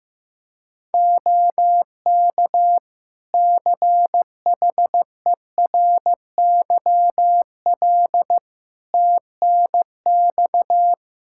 Morse code, heard as a keyed tone, about 15 wpm, 700 Hz. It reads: OK CHERYL TNX
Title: Practice QSO - Exchange 4 - Ending - 15wpm